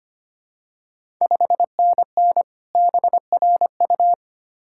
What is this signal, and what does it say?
Morse code, heard as a keyed tone, about 25 wpm, 690 Hz. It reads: 5NN BRU
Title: Ontario QSO Party Exchange 25wpm